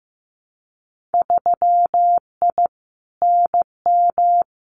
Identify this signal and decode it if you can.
Morse code, heard as a keyed tone, about 15 wpm, 700 Hz. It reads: 3I NM
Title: Winter Field Day Exchanges Encore 15wpm